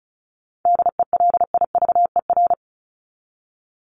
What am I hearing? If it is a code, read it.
Morse code, heard as a keyed tone, about 34 wpm, 700 Hz. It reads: DELIVER
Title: Top 1751-2000 Words 35wpm